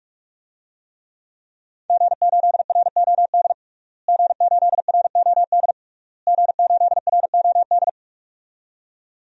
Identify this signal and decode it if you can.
Morse code, heard as a keyed tone, about 42 wpm, 700 Hz. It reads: G8ROD G8ROD G8ROD
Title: DXCC-100 Call Signs - 5 Characters - Repeated 3x - 45wpm